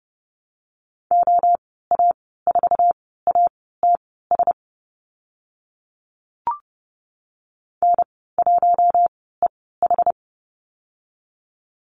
Morse code, heard as a keyed tone, about 30 wpm, spaced as for 17 wpm, 700 Hz. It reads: OA4ATS N1EH
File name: Lesson 14g - Number 2 - Call Signs 30wpm